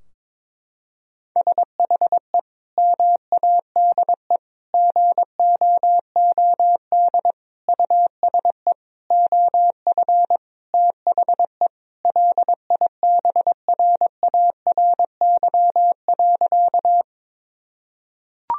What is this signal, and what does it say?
Morse code, heard as a keyed tone, about 22 wpm, 700 Hz. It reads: SHE MADE GOOD USE OF THE LIBRARY.
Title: Sentences from Top 1000 Words - Review Part 4 - 22wpm